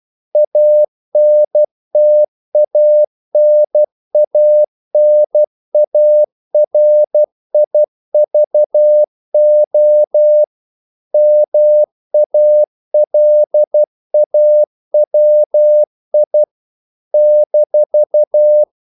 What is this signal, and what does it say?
Morse code, heard as a keyed tone, about 12 wpm, 600 Hz. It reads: ANTANANARIVO MALAWI -
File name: CW / Practice / NationalCapitals12